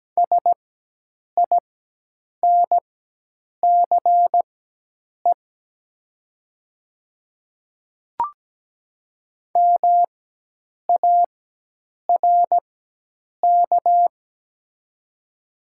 Morse code, heard as a keyed tone, about 17 wpm, spaced as for 8 wpm, 700 Hz. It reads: SINCE MARK
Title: 4x Character Spacing - Top 300 Words - 17wpm